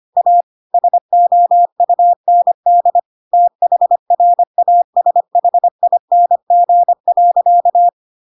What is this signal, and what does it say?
Morse code, heard as a keyed tone, about 25 wpm, 700 Hz. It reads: A SOUND THRASHING.
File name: Pinocchio / Pinocchio ch2 25 25 ik1hge